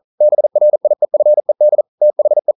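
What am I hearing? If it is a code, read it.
Morse code, heard as a keyed tone, about 40 wpm, 600 Hz. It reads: BRIEFED THE
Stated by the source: Morse code practice at 41 wpm, from CW med Gnister.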